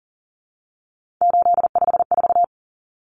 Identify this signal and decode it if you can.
Morse code, heard as a keyed tone, about 40 wpm, 700 Hz. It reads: 854
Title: Common Cut Numbers - 3-Digit 40wpm